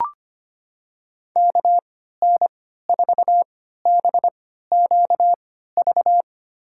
Morse code, heard as a keyed tone, about 25 wpm, spaced as for 14 wpm, 700 Hz. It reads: KN4BQV
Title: USA Call Signs - 3x Character Spacing - 25wpm